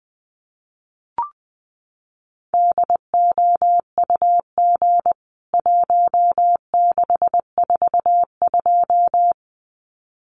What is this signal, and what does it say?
Morse code, heard as a keyed tone, about 20 wpm, 700 Hz. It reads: DOUG 1642
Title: CWT Exchange 20wpm